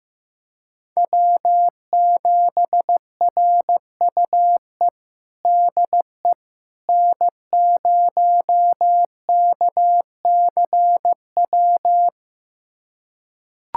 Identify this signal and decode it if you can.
Morse code, heard as a keyed tone, about 15 wpm, 700 Hz. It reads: W7RUE DE N0KCW
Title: USA Callsign DE Callsign - 5 Characters - 15wpm